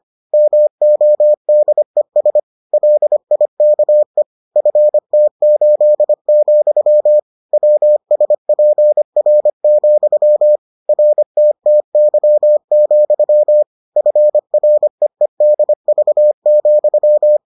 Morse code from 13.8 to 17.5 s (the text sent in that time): FREEDV,